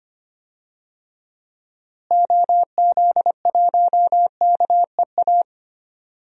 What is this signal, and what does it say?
Morse code, heard as a keyed tone, about 25 wpm, 700 Hz. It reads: OZ1KEA
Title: DXCC-100 Call Signs - 6 Characters - Repeated 2x - 25wpm